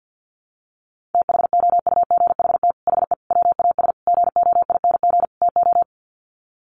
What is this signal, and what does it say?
Morse code, heard as a keyed tone, about 50 wpm, 700 Hz. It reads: THOUGHT HE WAS GOING TO
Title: Sets of 5 Words Encore2 50wpm